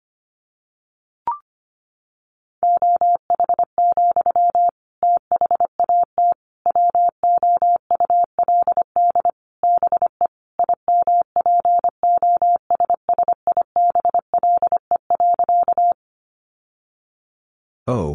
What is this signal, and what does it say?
Morse code, heard as a keyed tone, about 25 wpm, 700 Hz. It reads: OH, THAT WOULD BE IMPOSSIBLE.
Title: Sentences from Top 1100 Words 25wpm